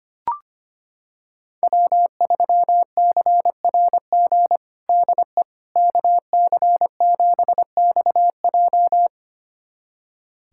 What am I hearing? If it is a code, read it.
Morse code, heard as a keyed tone, about 25 wpm, 700 Hz. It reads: W3CRG DE KC7XJ
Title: USA Callsign DE Callsign - 5 Characters - 25wpm